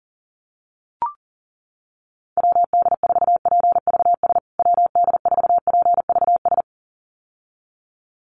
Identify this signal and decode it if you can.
Morse code, heard as a keyed tone, about 40 wpm, 700 Hz. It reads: WD4PVS WD4PVS
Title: USA Call Signs Repeated 2x 40wpm